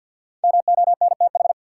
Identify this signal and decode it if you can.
Morse code, heard as a keyed tone, about 53 wpm, 700 Hz. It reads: MONTH
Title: Sets of 5 Words Encore - New Word Drill - 50wpm